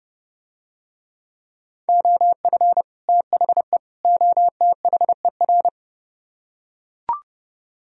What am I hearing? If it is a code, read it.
Morse code, heard as a keyed tone, about 30 wpm, 700 Hz. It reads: OF THE OTHER